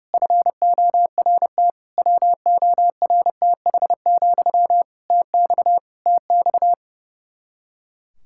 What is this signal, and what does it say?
Morse code, heard as a keyed tone, about 32 wpm, 700 Hz. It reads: FORT WORTH, TX TX
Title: Practice QSO - Exchange 1 - Beginning - 30wpm